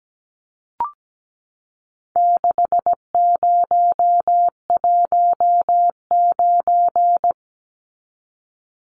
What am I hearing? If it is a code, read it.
Morse code, heard as a keyed tone, about 17 wpm, 700 Hz. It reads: 6019
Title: Numbers - 4-Digit 17wpm